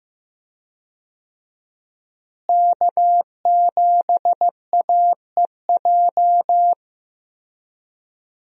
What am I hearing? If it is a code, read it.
Morse code, heard as a keyed tone, about 15 wpm, 700 Hz. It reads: K7AEJ
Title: USA Call Signs - 5 Characters - Repeated 2x - 15wpm